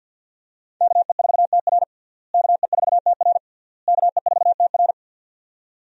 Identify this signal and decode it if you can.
Morse code, heard as a keyed tone, about 51 wpm, 700 Hz. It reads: KE4TR KE4TR KE4TR
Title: USA Call Signs - 5 Characters - Repeated 3x - 50wpm